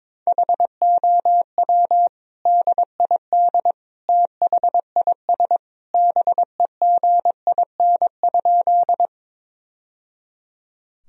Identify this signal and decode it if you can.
Morse code, heard as a keyed tone, about 22 wpm, 700 Hz. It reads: HOW DID THIS BEGIN?
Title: Sentences from Top 1000 Words - Review Part 1 - 22wpm